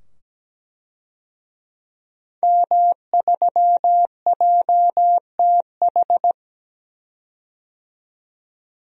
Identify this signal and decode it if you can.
Morse code, heard as a keyed tone, about 17 wpm, 700 Hz. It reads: M3JTH
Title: DXCC-100 Call Signs - 5 Characters - 17wpm